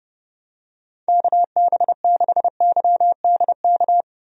Morse code, heard as a keyed tone, about 31 wpm, 700 Hz. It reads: KB6YDK
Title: USA Call Signs - 6 Characters - 30wpm